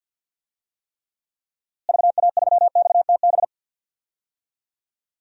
Morse code, heard as a keyed tone, about 52 wpm, 700 Hz. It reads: VA3XTB
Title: DXCC-100 Call Signs - 6 Characters - Repeated 3x - 50wpm